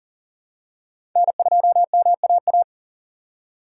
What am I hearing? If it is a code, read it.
Morse code, heard as a keyed tone, about 40 wpm, 700 Hz. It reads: N2MAA
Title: USA Call Signs - 4x Character Spacing - 40wpm